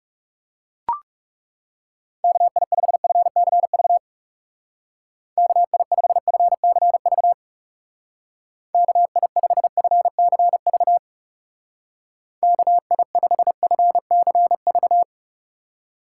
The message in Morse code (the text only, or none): KI5FCV KI5FCV KI5FCV KI5FCV